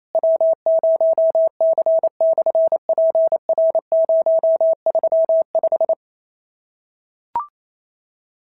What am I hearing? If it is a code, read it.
Morse code, heard as a keyed tone, about 28 wpm, 650 Hz. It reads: W0C/PR035
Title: SOTA QSO Practice 28wpm